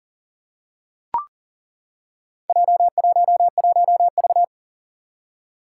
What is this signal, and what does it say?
Morse code, heard as a keyed tone, about 38 wpm, 700 Hz. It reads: J11V
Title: Four Challenging Letter-Number Combinations 40wpm